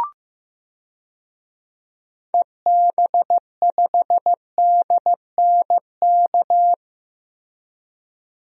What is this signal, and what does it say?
Morse code, heard as a keyed tone, about 15 wpm, 700 Hz. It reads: EB5DNK
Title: DXCC-100 Call Signs - 6 Characters - 15wpm